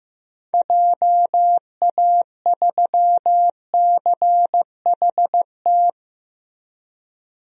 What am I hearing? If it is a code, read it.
Morse code, heard as a keyed tone, about 15 wpm, 700 Hz. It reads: JA3CHT
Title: DXCC-100 Call Signs - 6 Characters - Repeated 3x - 15wpm